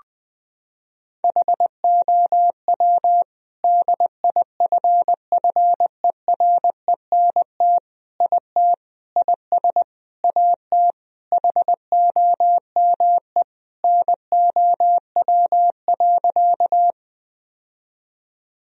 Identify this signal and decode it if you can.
Morse code, heard as a keyed tone, about 20 wpm, 700 Hz. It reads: HOW DIFFERENT IT IS AT HOME NOW.
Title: Sentences from Top 1000 Words - Review Part 3 - 20wpm